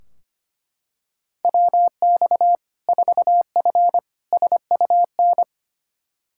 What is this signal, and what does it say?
Morse code, heard as a keyed tone, about 25 wpm, 700 Hz. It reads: WX 4F SUN
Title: Practice QSO - Exchange 2 - Info - 25wpm